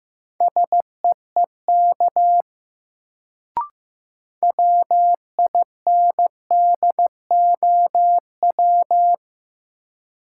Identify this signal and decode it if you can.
Morse code, heard as a keyed tone, about 15 wpm, 700 Hz. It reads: SEEK WINDOW
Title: Sentences from Top 1600 Words - New Word Drill Alt - 15wpm